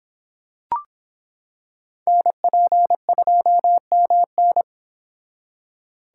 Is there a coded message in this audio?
Yes: Morse code from 2.1 to 4.6 s, heard as a keyed tone, about 25 wpm, 700 Hz. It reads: NP2MN